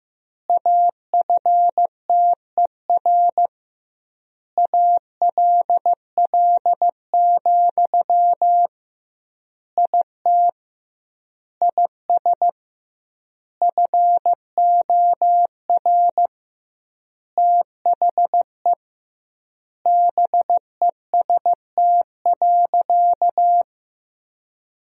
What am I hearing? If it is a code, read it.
Morse code, heard as a keyed tone, about 15 wpm, 700 Hz. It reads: AFTER ALL, IT IS FOR THE BEST.
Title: 2x Word-Spacing - Sentences from Top 500 Words - 15wpm